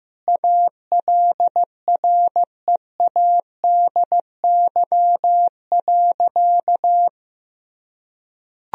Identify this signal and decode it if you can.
Morse code, heard as a keyed tone, about 15 wpm, 700 Hz. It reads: ALREADY.
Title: Sentences from Top 1000 Words - Review Part 3 - 15wpm